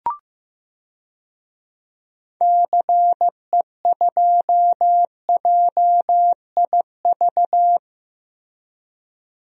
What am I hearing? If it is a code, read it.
Morse code, heard as a keyed tone, about 15 wpm, 700 Hz. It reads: CE2JIV
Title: DXCC-100 Call Signs - 6 Characters - Repeated 3x - 15wpm